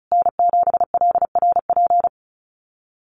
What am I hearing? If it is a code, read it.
Morse code, heard as a keyed tone, about 34 wpm, 700 Hz. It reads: N7LRP
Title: USA Call Signs Repeated 2x 35wpm